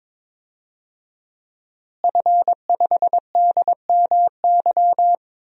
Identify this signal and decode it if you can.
Morse code, heard as a keyed tone, about 21 wpm, 700 Hz. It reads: F5DMY
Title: DXCC-100 Call Signs - 5 Characters - Repeated 2x - 22wpm